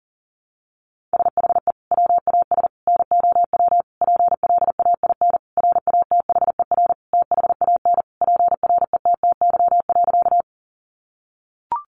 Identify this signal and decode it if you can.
Morse code, heard as a keyed tone, about 40 wpm, 700 Hz. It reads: SHE WAS NOW PLAIN RATHER THAN PRETTY.
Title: Sentences from Top 1000 Words 40wpm